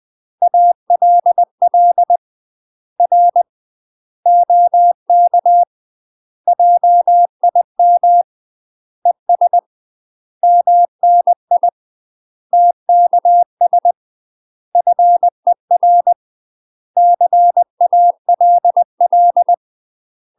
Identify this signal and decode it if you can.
Morse code, heard as a keyed tone, about 20 wpm, 700 Hz. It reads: ALL R OK JIM ES MNI TKS FER CALL
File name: UK MASTER QSOs TRAINING / 20WPM / UK MASTER QSOs 0064